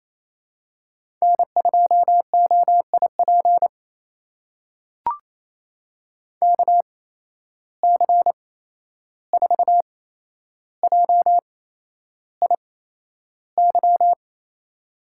Morse code, heard as a keyed tone, about 28 wpm, spaced as for 11 wpm, 700 Hz. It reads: N2OIP KC4JIY
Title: USA Call Signs - 8x Character Spacing - 28wpm